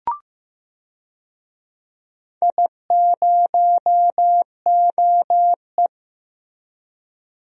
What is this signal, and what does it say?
Morse code, heard as a keyed tone, about 15 wpm, 700 Hz. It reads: I0OE
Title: DXCC-100 Call Signs - 4 Characters - Repeated 2x - 15wpm